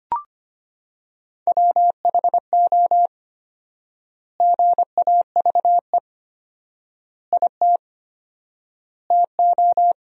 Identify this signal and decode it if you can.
Morse code, heard as a keyed tone, about 25 wpm, 700 Hz. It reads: WHO GAVE IT TO
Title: 4x Word-Spacing - Sentences from Top 500 Words - 25wpm